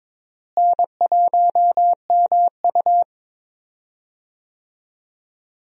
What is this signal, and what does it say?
Morse code, heard as a keyed tone, about 22 wpm, 700 Hz. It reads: N1MU
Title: USA Call Signs - 4 Characters - Repeated 2x - 22wpm